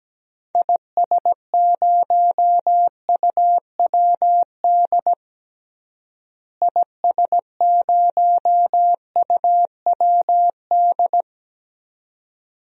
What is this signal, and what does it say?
Morse code, heard as a keyed tone, about 17 wpm, 700 Hz. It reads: IS0UWD IS0UWD
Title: DXCC-100 Call Signs - 6 Characters - Repeated 2x - 17wpm